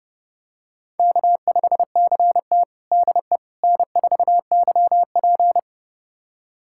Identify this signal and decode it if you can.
Morse code, heard as a keyed tone, about 30 wpm, 700 Hz. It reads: K5CT DE N4YP
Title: USA Callsign DE Callsign - 4 Characters - 30wpm